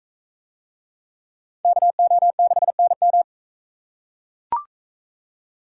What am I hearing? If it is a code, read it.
Morse code, heard as a keyed tone, about 44 wpm, 700 Hz. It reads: KO6NM